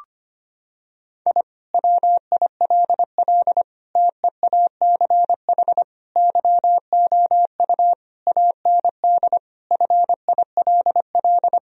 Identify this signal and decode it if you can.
Morse code, heard as a keyed tone, about 25 wpm, 700 Hz. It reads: I WILL TEACH YOU AND FILL
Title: Sentences from Top 1000 Words - Review Part 3 - 25wpm